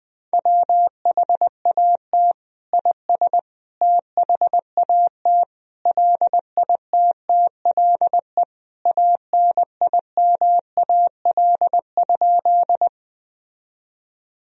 Morse code, heard as a keyed tone, about 20 wpm, 700 Hz. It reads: WHAT IS THAT LITTLE ANIMAL?
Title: Sentences from Top 1000 Words - Review Part 1 - 20wpm